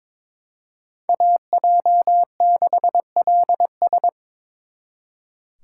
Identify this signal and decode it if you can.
Morse code, heard as a keyed tone, about 26 wpm, 700 Hz. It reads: AJ6LS
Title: Speed Racing - USA 2x2 Call Signs 22wpm